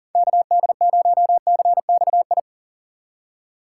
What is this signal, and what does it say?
Morse code, heard as a keyed tone, about 40 wpm, 700 Hz. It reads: KD0CXI
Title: USA Call Signs - 6 Characters - 4x Character Spacing - 40wpm